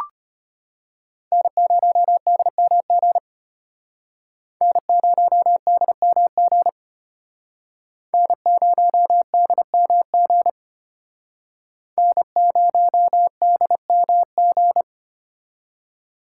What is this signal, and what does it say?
Morse code, heard as a keyed tone, about 32 wpm, 700 Hz. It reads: N0DMG N0DMG N0DMG N0DMG